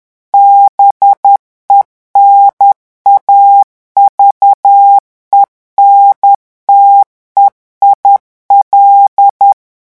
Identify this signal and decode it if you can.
Morse code, heard as a keyed tone, about 10 wpm, 800 Hz. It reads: BENAVENTE IL
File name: revision9 citations humoristiques vitesse 10 mots minute